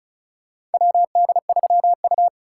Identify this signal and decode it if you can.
Morse code, heard as a keyed tone, about 36 wpm, 700 Hz. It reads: WD3U
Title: USA Call Signs - 4 Characters - 4x Character Spacing - 35wpm